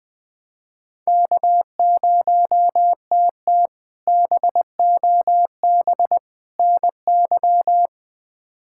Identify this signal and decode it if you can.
Morse code, heard as a keyed tone, about 20 wpm, 700 Hz. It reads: K0TT BOB NY
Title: K1USN SST QSO Practice - 20wpm